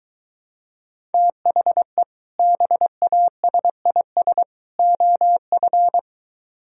Morse code, heard as a keyed tone, about 24 wpm, 700 Hz. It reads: THE BASIS OF